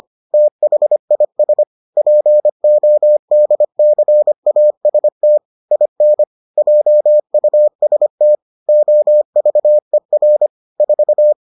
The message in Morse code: THIS PODCAST IN JUST OVER 4

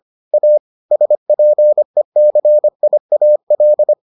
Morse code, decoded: A SPECIAL

